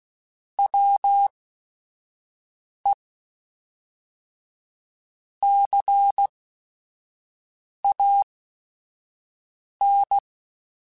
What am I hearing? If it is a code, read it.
Morse code, heard as a keyed tone, about 16 wpm, spaced as for 5 wpm, 800 Hz. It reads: WE CAN
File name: QSOs / QSOs-mp3s / WA7QOX 1@5